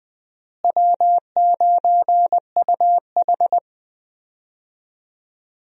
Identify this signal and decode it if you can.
Morse code, heard as a keyed tone, about 20 wpm, 700 Hz. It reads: W9UH
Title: USA Call Signs - 4 Characters - Repeated 2x - 20wpm